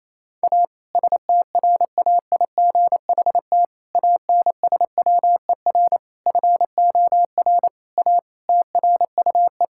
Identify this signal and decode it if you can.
Morse code, heard as a keyed tone, about 28 wpm, 700 Hz. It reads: A STRAIGHT ANSWER FOR A TRUE
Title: Sentences from Top 1000 Words - Review Part 6 - 28wpm